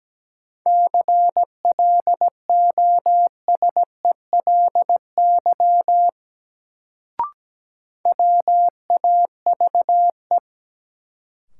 Morse code, heard as a keyed tone, about 17 wpm, 700 Hz. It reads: CLOSELY WAVE
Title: Sentences from Top 1400 Words - New Word Drill Alt - 17wpm